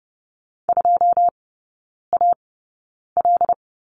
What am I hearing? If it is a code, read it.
Morse code, heard as a keyed tone, about 30 wpm, spaced as for 9 wpm, 700 Hz. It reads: 2AL
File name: 081 Lesson 14e - Number 2 - Three Characters 30wpm